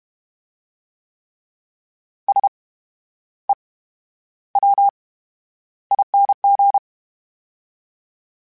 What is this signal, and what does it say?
Morse code, heard as a keyed tone, about 32 wpm, 800 Hz. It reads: S E W ING
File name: Special INGs / Special INGs-mp3s / VerySlowToFaster Ing @30